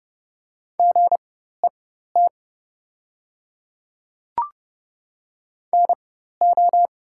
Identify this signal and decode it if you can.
Morse code, heard as a keyed tone, about 31 wpm, spaced as for 14 wpm, 700 Hz. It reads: GET NO
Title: 4x Character Spacing - Top 200 Words - 30wpm